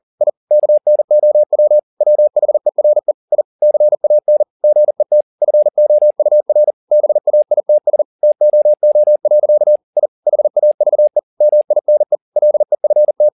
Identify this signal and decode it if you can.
Morse code, heard as a keyed tone, about 40 wpm, 600 Hz. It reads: I KNOW WHERE I CAN GET FOUR BAITS TOO. I HAVE MINE LEFT
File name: the old man and the sea 1 40wpm